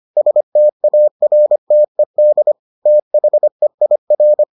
Morse code, heard as a keyed tone, about 25 wpm, 600 Hz. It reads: STARTED THEIR